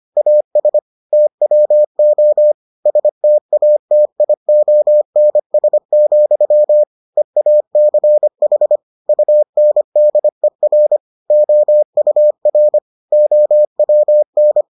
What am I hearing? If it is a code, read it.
Morse code, heard as a keyed tone, about 25 wpm, 600 Hz. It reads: AS TWO STATIONS, EACH UNDER OUR OWN